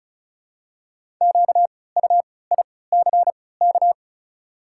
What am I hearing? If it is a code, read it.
Morse code, heard as a keyed tone, about 36 wpm, spaced as for 20 wpm, 700 Hz. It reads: QUICK
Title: 3x Character Spacing - Top 500 Words - 35wpm